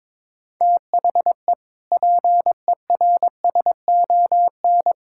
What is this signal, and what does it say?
Morse code, heard as a keyed tone, about 22 wpm, 700 Hz. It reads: THE PERSON
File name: Sentences with 7-Word Phrases - 22wpm